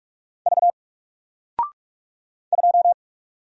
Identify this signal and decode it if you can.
Morse code, heard as a keyed tone, about 44 wpm, 700 Hz. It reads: V 2
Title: Single Letter-Number 45wpm